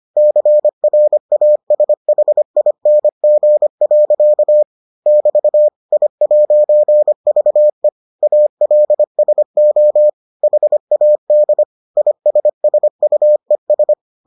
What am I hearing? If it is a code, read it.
Morse code, heard as a keyed tone, about 25 wpm, 600 Hz. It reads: CRASHING. <BT> I'VE ALSO HAD ISSUES